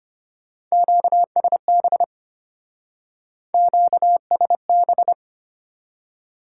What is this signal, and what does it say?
Morse code, heard as a keyed tone, about 30 wpm, 700 Hz. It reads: QSB QSB
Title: Speed-Racing Top 90 QSO Elements 25wpm